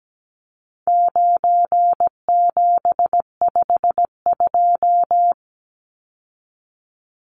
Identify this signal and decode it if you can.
Morse code, heard as a keyed tone, about 17 wpm, 700 Hz. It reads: 9752
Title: Numbers - 4-Digit 17wpm